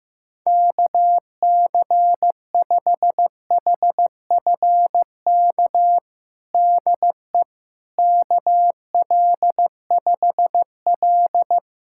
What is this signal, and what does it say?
Morse code, heard as a keyed tone, about 15 wpm, 700 Hz. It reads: KC5HFK DE KL5L
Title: USA Callsign DE Callsign 15wpm